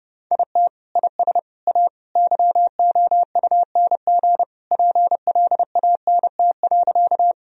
Morse code, heard as a keyed tone, about 30 wpm, 700 Hz. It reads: IT IS A YOUNG PLANT.
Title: Sentences from Top 1000 Words - Review Part 1 - 30wpm